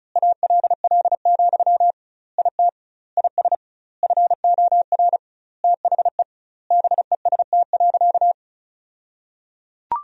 Morse code, heard as a keyed tone, about 34 wpm, 700 Hz. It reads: ALL, IT IS FOR THE BEST.
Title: 2x Word-Spacing - Sentences from Top 500 Words - 35wpm